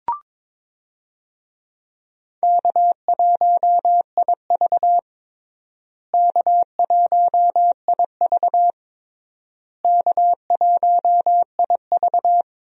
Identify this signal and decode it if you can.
Morse code, heard as a keyed tone, about 22 wpm, 700 Hz. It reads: K1IV K1IV K1IV
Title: USA Call Signs - 4 Characters - Repeated 3x - 22wpm